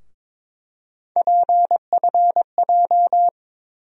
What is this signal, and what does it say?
Morse code, heard as a keyed tone, about 22 wpm, 700 Hz. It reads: PFJ
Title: USA Call Sign Postfixes - 3 Letters 22wpm